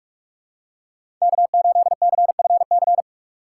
Morse code, heard as a keyed tone, about 44 wpm, 700 Hz. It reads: K 8 C F C
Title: USA Call Signs - 5 Characters - 6x Character Spacing - 45wpm